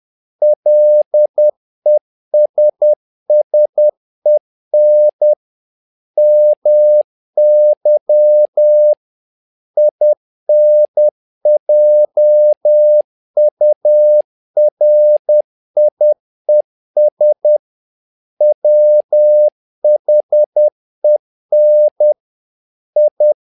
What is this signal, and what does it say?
Morse code, heard as a keyed tone, about 10 wpm, 600 Hz. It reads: LESSEN MY INJURIES WHEN I
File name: learning center / tower trouble / tower trouble 10wpm